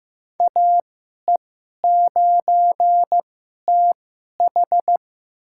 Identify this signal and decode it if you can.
Morse code, heard as a keyed tone, about 15 wpm, spaced as for 11 wpm, 700 Hz. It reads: AE9TH
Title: USA Call Signs - 5 Characters - 2x Character Spacing - 15wpm